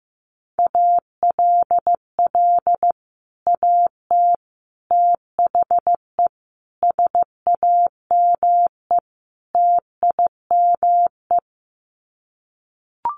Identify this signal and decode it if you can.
Morse code, heard as a keyed tone, about 15 wpm, 700 Hz. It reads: ALL AT THE SAME TIME
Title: Sets of 5 Words Encore2 15wpm